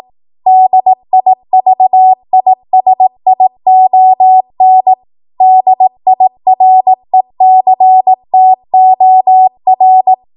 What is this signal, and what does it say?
Morse code, heard as a keyed tone, about 18 wpm, 750 Hz. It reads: DIVISION DIRECTOR